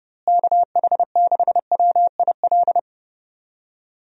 Morse code, heard as a keyed tone, about 30 wpm, 700 Hz. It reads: KH6WIL
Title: USA Call Signs - 6 Characters - Repeated 3x - 30wpm